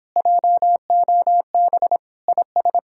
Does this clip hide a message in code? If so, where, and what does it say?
Yes: Morse code from 0.2 to 2.8 s, heard as a keyed tone, about 26 wpm, 700 Hz. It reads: JOB IS